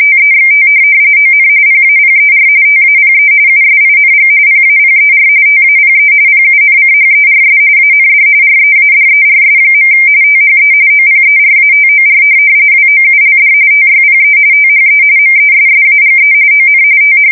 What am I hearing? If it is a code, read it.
RTTY, about 45.45 baud. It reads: TEXT ON PAPER TAPE USING PAPER TAPE PUNCHERS AND READERS. THE OPERATOR WOULD TYPE THE MESSAGE ON TH
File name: fsk170 / teletype-RTTY